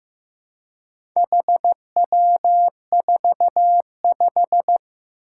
Morse code, heard as a keyed tone, about 15 wpm, 700 Hz. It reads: HW45
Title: Four Challenging Letter-Number Combinations 15wpm